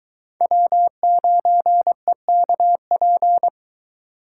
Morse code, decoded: W9EKP

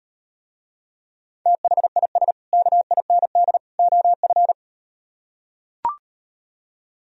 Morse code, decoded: THIS KIND OF

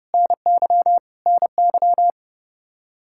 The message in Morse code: NY NY

